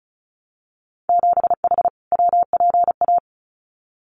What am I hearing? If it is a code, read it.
Morse code, heard as a keyed tone, about 36 wpm, 700 Hz. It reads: 7H WPA
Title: Winter Field Day Exchanges Encore 35wpm